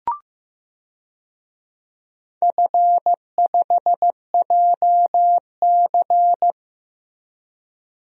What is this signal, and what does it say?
Morse code, heard as a keyed tone, about 15 wpm, 700 Hz. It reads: F5JC
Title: DXCC-100 Call Signs - 4 Characters - Repeated 3x - 15wpm